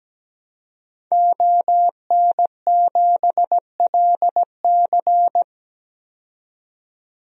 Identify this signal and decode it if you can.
Morse code, heard as a keyed tone, about 17 wpm, 700 Hz. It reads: ON7LC
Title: DXCC-100 Call Signs - 5 Characters - 17wpm